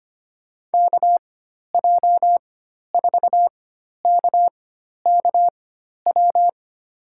Morse code, heard as a keyed tone, about 25 wpm, spaced as for 12 wpm, 700 Hz. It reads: KJ4KKW
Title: USA Call Signs - 6 Characters - 4x Character Spacing - 25wpm